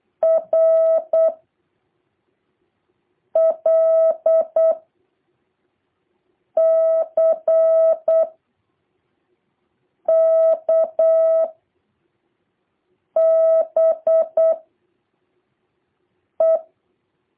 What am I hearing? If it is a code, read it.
Morse code, heard as a keyed tone, about 8 wpm, 650 Hz. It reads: R L C K B E
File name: jh8cbh / 2023cw